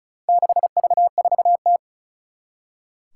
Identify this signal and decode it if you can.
Morse code, heard as a keyed tone, about 34 wpm, 700 Hz. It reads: 6V4T
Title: Four Challenging Letter-Number Combinations 35wpm